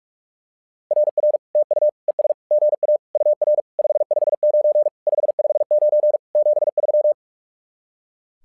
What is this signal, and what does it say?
Morse code, heard as a keyed tone, about 46 wpm, 600 Hz. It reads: RR TU ES GA UR 559 559 73
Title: SOTA QSO Practice 45wpm